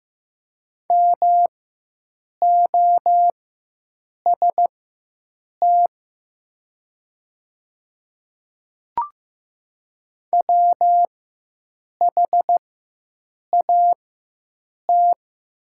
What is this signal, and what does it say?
Morse code, heard as a keyed tone, about 15 wpm, spaced as for 7 wpm, 700 Hz. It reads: MOST WHAT
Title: 4x Character Spacing - Top 200 Words - 15wpm